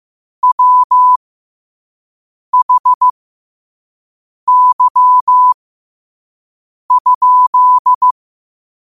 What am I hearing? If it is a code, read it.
Morse code, heard as a keyed tone, about 15 wpm, spaced as for 5 wpm, 1000 Hz. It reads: WHY?
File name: qotdfiles / 5wpm